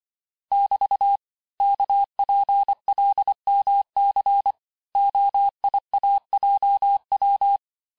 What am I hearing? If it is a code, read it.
Morse code, heard as a keyed tone, about 25 wpm, 800 Hz. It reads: <BT> KPLMC OIAJW
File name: texte-chiffre-vitesse-23mpm